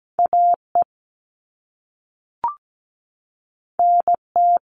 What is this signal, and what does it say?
Morse code, heard as a keyed tone, about 17 wpm, 700 Hz. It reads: AE NT
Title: All Cut Numbers - 2-Digit 17wpm